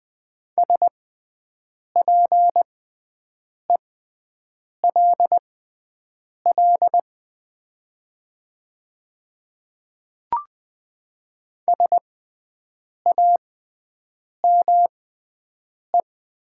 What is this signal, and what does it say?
Morse code, heard as a keyed tone, about 20 wpm, spaced as for 7 wpm, 700 Hz. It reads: SPELL SAME
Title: 6x Character Spacing - Top 200 Words - 20wpm